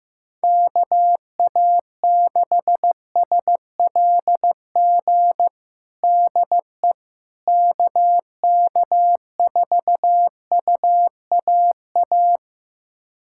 Morse code, heard as a keyed tone, about 15 wpm, 700 Hz. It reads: KA6SLG DE KK4UAA
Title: USA Callsign DE Callsign - 6 Characters - 15wpm